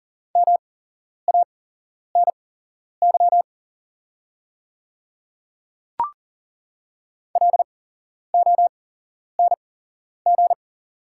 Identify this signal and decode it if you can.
Morse code, heard as a keyed tone, about 40 wpm, spaced as for 11 wpm, 700 Hz. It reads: MANY LONG
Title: 8x Character Spacing - Top 100 Words - 40wpm